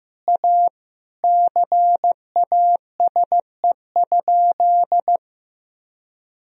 Morse code, heard as a keyed tone, about 15 wpm, 700 Hz. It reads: A CASE?
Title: Sentences from Top 1000 Words - Review Part 2 - 15wpm